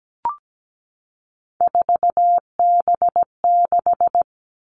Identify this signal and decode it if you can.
Morse code, heard as a keyed tone, about 17 wpm, 700 Hz. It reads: 4B6
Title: Three Challenging Letter-Number Combinations 17wpm